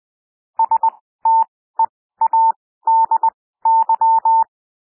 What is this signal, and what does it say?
Morse code, heard as a keyed tone, about 20 wpm, spaced as for 14 wpm, 900 Hz. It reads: STEADY
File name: SOUND WORDS 20 wpm